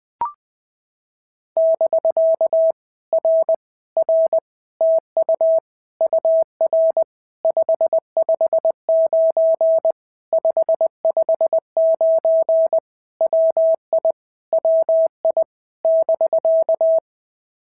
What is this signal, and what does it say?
Morse code, heard as a keyed tone, about 20 wpm, 650 Hz. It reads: <BK> R R TU UR 559 559 WI WI <BK>
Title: POTA QSO Practice - 20wpm